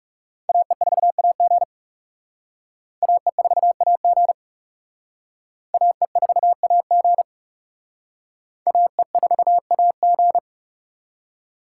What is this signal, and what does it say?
Morse code, heard as a keyed tone, about 35 wpm, 700 Hz. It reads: AE4AG AE4AG AE4AG AE4AG
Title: Speed Racing - USA Call Signs 30wpm